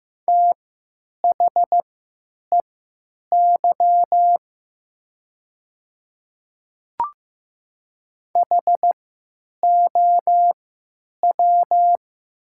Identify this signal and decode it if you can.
Morse code, heard as a keyed tone, about 15 wpm, spaced as for 9 wpm, 700 Hz. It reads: THEY HOW
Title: 3x Character Spacing - Top 300 Words - 15wpm